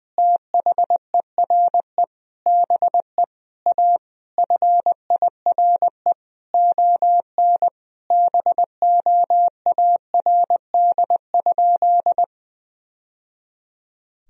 Morse code, heard as a keyed tone, about 20 wpm, 700 Hz. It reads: THERE BE A FIRE ON BOARD?
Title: Sentences from Top 1000 Words - Review Part 4 - 20wpm